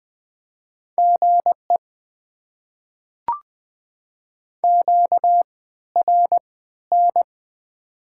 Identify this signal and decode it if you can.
Morse code, heard as a keyed tone, about 20 wpm, spaced as for 13 wpm, 690 Hz. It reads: GE QRN
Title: Top 90 QSO Elements - 3x Character Spacing - 20wpm